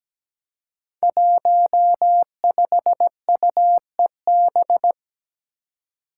Morse code, heard as a keyed tone, about 17 wpm, 700 Hz. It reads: 15UEB
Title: Five Challenging Letter-Number Combinations 17wpm